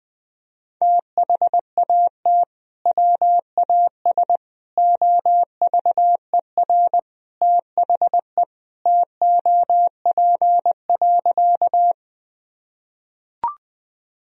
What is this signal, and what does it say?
Morse code, heard as a keyed tone, about 20 wpm, 700 Hz. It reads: THAT WAS OVER THE TOP.
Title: Sentences from Top 1000 Words - Review Part 2 - 20wpm